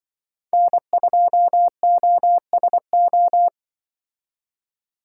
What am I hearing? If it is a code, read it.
Morse code, heard as a keyed tone, about 25 wpm, 700 Hz. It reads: N2OSO
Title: Speed Racing - USA 1x3 Call Signs 20wpm